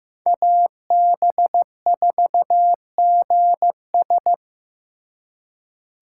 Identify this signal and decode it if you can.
Morse code, heard as a keyed tone, about 18 wpm, 700 Hz. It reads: AB4GS
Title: Speed Racing - USA 2x2 Call Signs 15wpm